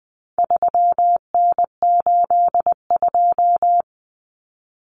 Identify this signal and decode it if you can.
Morse code, heard as a keyed tone, about 20 wpm, 700 Hz. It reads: 3N82
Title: Common Cut Numbers - 4-Digit 20wpm